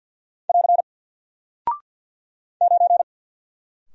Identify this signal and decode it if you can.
Morse code, heard as a keyed tone, about 48 wpm, 700 Hz. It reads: <AR> 9
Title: Single Character (Letter:Number:Punctuation:Prosign) Alt 50wpm